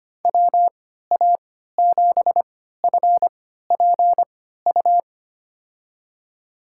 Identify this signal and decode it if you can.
Morse code, heard as a keyed tone, about 25 wpm, spaced as for 14 wpm, 700 Hz. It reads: WA7FPU
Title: USA Call Signs - 6 Characters - 3x Character Spacing - 25wpm